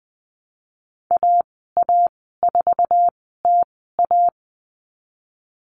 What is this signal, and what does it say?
Morse code, heard as a keyed tone, about 20 wpm, spaced as for 14 wpm, 700 Hz. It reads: AA4TA
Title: 048 Lesson 9g - Number 4 - Call Signs 20wpm